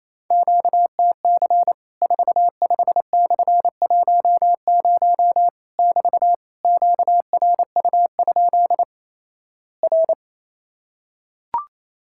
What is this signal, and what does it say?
Morse code, heard as a keyed tone, about 28 wpm, 700 Hz. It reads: QTC 45/10 <BT> QRU? R
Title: Worked All Europe DX Contest - QTC Exchange - 28wpm